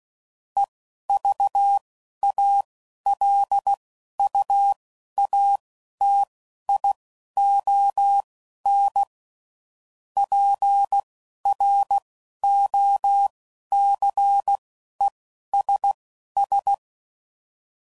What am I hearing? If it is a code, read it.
Morse code, heard as a keyed tone, about 16 wpm, spaced as for 11 wpm, 800 Hz. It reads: EVALUATION PROCESS
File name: CW Code Groups @10WPM / Technical Subjects / AND Gate